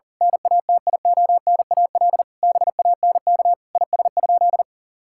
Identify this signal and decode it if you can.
Morse code, heard as a keyed tone, about 37 wpm, 700 Hz. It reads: NATIONAL BANK IS?
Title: Sentences from Top 1000 Words - Review Part 6 - 40wpm